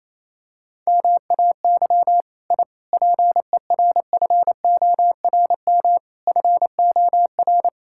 Morse code, heard as a keyed tone, about 28 wpm, 700 Hz. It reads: MAY I PERFORM FOR
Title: Sentences from Top 1000 Words - Review Part 5 - 28wpm